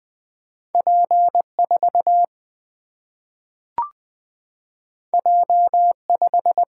Morse code, heard as a keyed tone, about 20 wpm, 700 Hz. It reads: P4 J5
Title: Single-Letter Single-Number 20wpm